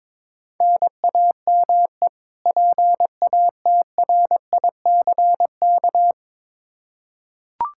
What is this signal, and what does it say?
Morse code, heard as a keyed tone, about 22 wpm, 700 Hz. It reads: NAME PATRICK
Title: K1USN SST QSO Practice 22wpm